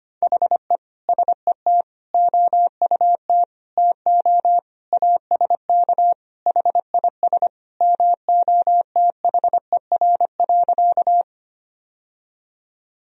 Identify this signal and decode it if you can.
Morse code, heard as a keyed tone, about 25 wpm, 700 Hz. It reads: HE SET OUT TO ASK HIS MOTHER.
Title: Sentences from Top 1000 Words - Review Part 1 - 25wpm